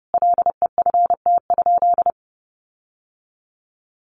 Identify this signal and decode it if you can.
Morse code, heard as a keyed tone, about 30 wpm, 700 Hz. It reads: LEFT?
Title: Sentences from Top 1500 Words 30wpm